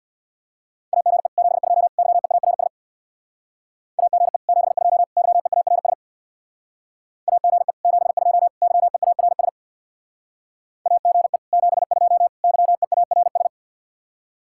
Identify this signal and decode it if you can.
Morse code, heard as a keyed tone, about 63 wpm, 700 Hz. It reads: AGE 72 YEARS AGE 72 YEARS AGE 72 YEARS AGE 72 YEARS